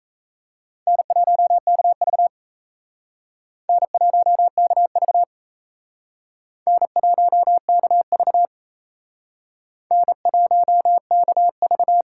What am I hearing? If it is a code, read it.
Morse code, heard as a keyed tone, about 38 wpm, 700 Hz. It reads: N1KV N1KV N1KV N1KV